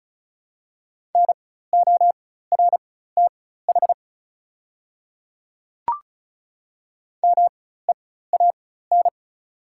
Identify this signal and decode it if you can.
Morse code, heard as a keyed tone, about 34 wpm, spaced as for 16 wpm, 700 Hz. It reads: NORTH MEAN
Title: 4x Character Spacing - Top 300 Words - 35wpm